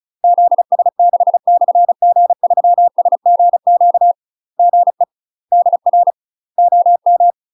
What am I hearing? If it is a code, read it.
Morse code, heard as a keyed tone, about 33 wpm, 700 Hz. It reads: ZS6/G3SGQ GE DR OM
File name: UK MASTER QSOs TRAINING / 35WPM / UK MASTER QSOs 0017